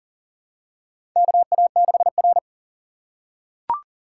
Morse code, heard as a keyed tone, about 41 wpm, 700 Hz. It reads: KA6R
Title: USA Call Signs - 4 Characters - 6x Character Spacing - 40wpm